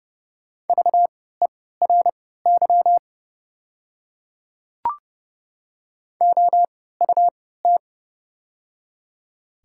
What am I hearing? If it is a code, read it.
Morse code, heard as a keyed tone, about 31 wpm, spaced as for 17 wpm, 700 Hz. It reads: VERY OUT
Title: 3x Character Spacing - Top 200 Words - 30wpm